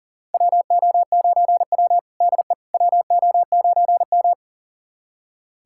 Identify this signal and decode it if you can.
Morse code, heard as a keyed tone, about 38 wpm, 700 Hz. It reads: WO9W DE WO9M
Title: USA Callsign DE Callsign - 4 Characters - 40wpm